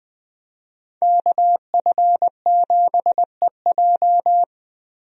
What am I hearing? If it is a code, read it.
Morse code, heard as a keyed tone, about 19 wpm, 700 Hz. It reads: KF7EJ
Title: USA Call Signs - 5 Characters - Repeated 3x - 20wpm